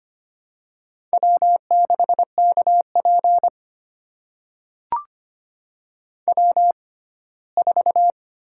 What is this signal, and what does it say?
Morse code, heard as a keyed tone, about 25 wpm, spaced as for 17 wpm, 700 Hz. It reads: W6KP W4
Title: USA Call Signs - 6x Character Spacing - 25wpm